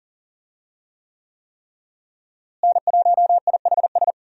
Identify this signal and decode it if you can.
Morse code, heard as a keyed tone, about 40 wpm, 700 Hz. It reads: N1IHS
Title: USA Call Signs - 5 Characters - Repeated 3x - 40wpm